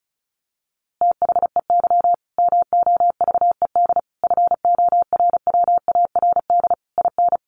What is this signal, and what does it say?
Morse code, heard as a keyed tone, about 34 wpm, 700 Hz. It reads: THEY MOVED FORWARD IN
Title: A Sound Of Thunder 35wpm